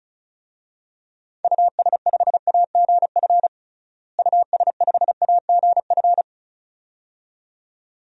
Morse code, heard as a keyed tone, about 35 wpm, 700 Hz. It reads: US5AGF US5AGF
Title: DXCC-100 Call Signs - 6 Characters - Repeated 2x - 35wpm